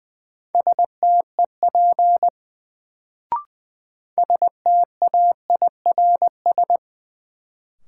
Morse code, heard as a keyed tone, about 20 wpm, 700 Hz. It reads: STEP STAIRS
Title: Sets of 5 Words Encore - New Word Drill - 20wpm